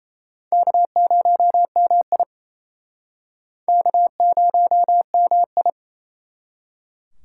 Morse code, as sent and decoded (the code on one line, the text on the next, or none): -.- ----- -- .. / -.- ----- -- ..
K0MI K0MI